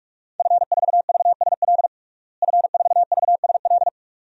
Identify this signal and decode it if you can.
Morse code, heard as a keyed tone, about 47 wpm, 700 Hz. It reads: F4VSL F4VSL
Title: DXCC-100 Call Signs - 5 Characters - Repeated 2x - 45wpm